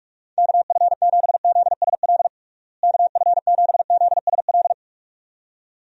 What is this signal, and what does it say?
Morse code, heard as a keyed tone, about 46 wpm, 700 Hz. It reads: KF7ZSL KF7ZSL
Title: USA Call Signs - 6 Characters - Repeated 2x - 45wpm